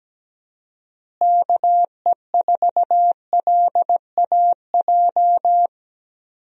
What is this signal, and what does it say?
Morse code, heard as a keyed tone, about 17 wpm, 700 Hz. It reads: KE4LAJ
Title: USA Call Signs - 6 Characters - Repeated 3x - 17wpm